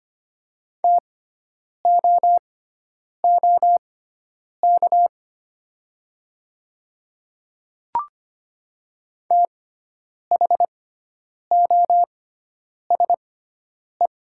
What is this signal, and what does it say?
Morse code, heard as a keyed tone, about 25 wpm, spaced as for 9 wpm, 700 Hz. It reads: TOOK THOSE
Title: 6x Character Spacing - Top 300 Words - 25wpm